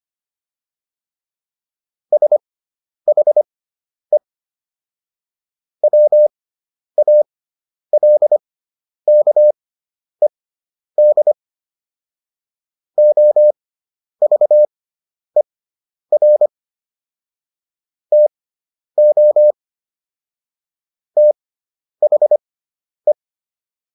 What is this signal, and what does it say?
Morse code, heard as a keyed tone, about 25 wpm, spaced as for 10 wpm, 600 Hz. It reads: SHE WALKED OVER TO THE